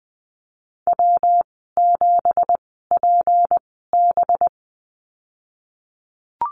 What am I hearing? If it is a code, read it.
Morse code, heard as a keyed tone, about 19 wpm, spaced as for 14 wpm, 700 Hz. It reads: W7PB
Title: 213 Lesson 34g - Number 8 - Call Signs 20wpm